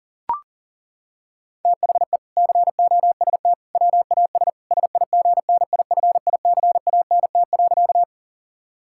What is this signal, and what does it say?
Morse code, heard as a keyed tone, about 40 wpm, 700 Hz. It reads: THE COST WAS SIGNIFICANT.
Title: Sentences from Top 1000 Words - Review Part 6 - 40wpm